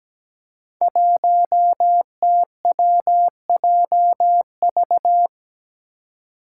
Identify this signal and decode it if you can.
Morse code, heard as a keyed tone, about 17 wpm, 700 Hz. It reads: 1TWJV
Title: Five Challenging Letter-Number Combinations 17wpm